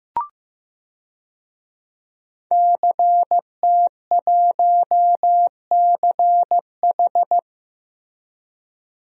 Morse code, heard as a keyed tone, about 15 wpm, 700 Hz. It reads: CT1CH
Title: DXCC-100 Call Signs - 5 Characters - 15wpm